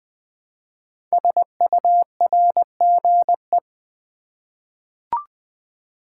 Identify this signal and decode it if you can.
Morse code, heard as a keyed tone, about 21 wpm, 700 Hz. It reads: SURGE
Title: 5 Letter Words - First 3 Letters Given - 20wpm